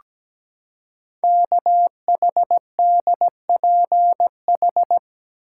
Morse code, heard as a keyed tone, about 17 wpm, 700 Hz. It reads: KHDPH
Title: Code Groups - 5 Letters - 17wpm